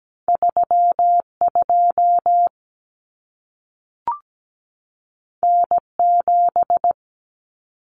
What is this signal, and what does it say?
Morse code, heard as a keyed tone, about 17 wpm, 700 Hz. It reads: 32 N7
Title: Common Cut Numbers - 2-Digit 17wpm